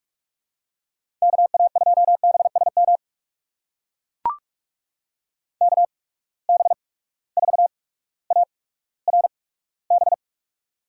Morse code, heard as a keyed tone, about 44 wpm, spaced as for 19 wpm, 700 Hz. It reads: KA2BSM KB4ARB
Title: USA Call Signs - 6 Characters - 8x Character Spacing - 45wpm